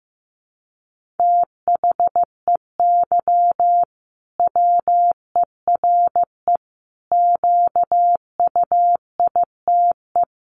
Morse code, heard as a keyed tone, about 15 wpm, 700 Hz. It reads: THEY WERE QUITE
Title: Sentences from Top 1500 Words 15wpm